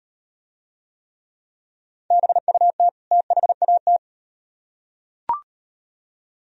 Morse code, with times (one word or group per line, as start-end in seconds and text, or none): 2.1-2.9 BUT
3.1-4.0 THAT